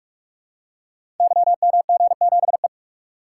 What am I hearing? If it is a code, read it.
Morse code, heard as a keyed tone, about 44 wpm, 700 Hz. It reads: YMG7E
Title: Code Groups - 5 Letter:Numbers - 45wpm